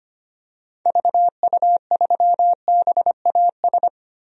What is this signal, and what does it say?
Morse code, heard as a keyed tone, about 25 wpm, 700 Hz. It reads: VU3BAS
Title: DXCC-100 Call Signs - 6 Characters - Repeated 2x - 25wpm